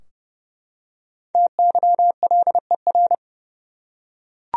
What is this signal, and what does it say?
Morse code, heard as a keyed tone, about 30 wpm, 700 Hz. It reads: TYLER
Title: USA Common Names - 30wpm